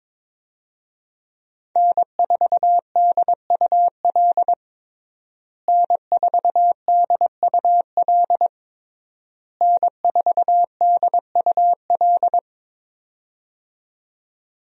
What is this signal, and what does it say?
Morse code, heard as a keyed tone, about 22 wpm, 700 Hz. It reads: N4DUL N4DUL N4DUL
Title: USA Call Signs - 5 Characters - Repeated 3x - 22wpm